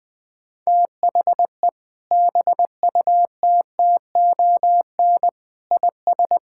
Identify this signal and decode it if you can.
Morse code, heard as a keyed tone, about 21 wpm, 700 Hz. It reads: THE BUTTON IS
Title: Sentences with 6-Word Phrases - 20wpm